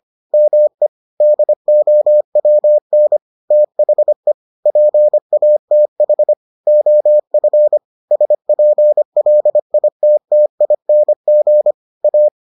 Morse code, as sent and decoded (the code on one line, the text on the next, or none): -- . / -.. --- .-- -. / - .... / .--. .- - .... / --- ..-. / ... .--. .-.. .. - - .. -. --. / .-
ME DOWN THE PATH OF SPLITTING A